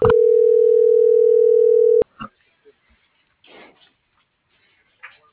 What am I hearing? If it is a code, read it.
Ringback tone (North America), 440 + 480 Hz.